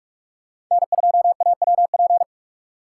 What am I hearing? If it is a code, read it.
Morse code, heard as a keyed tone, about 43 wpm, 700 Hz. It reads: N2AWP
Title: USA Call Signs - 5 Characters - 3x Character Spacing - 45wpm